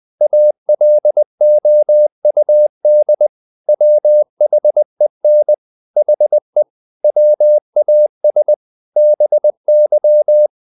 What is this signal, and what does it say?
Morse code, heard as a keyed tone, about 20 wpm, 600 Hz. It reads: ALOUD WHEN HE WAS BY
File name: the old man and the sea / the old man and the sea part 1 20wpm